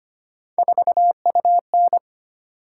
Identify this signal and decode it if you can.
Morse code, heard as a keyed tone, about 25 wpm, 700 Hz. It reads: 4UN
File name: Three Letter-Number Combinations - 25wpm